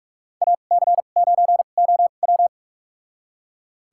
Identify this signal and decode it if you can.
Morse code, heard as a keyed tone, about 45 wpm, spaced as for 32 wpm, 700 Hz. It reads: AC9OW
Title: USA Call Signs - 5 Characters - 2x Character Spacing - 45wpm